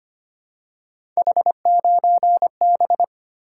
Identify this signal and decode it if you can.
Morse code, heard as a keyed tone, about 25 wpm, 700 Hz. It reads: H9B
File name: Three Letter-Number Combinations - 25wpm